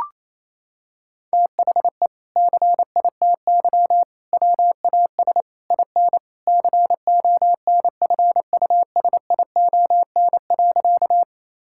Morse code, heard as a keyed tone, about 28 wpm, 700 Hz. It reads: THE CITY WAS IN CONFUSION.
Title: Sentences from Top 1600 Words 28wpm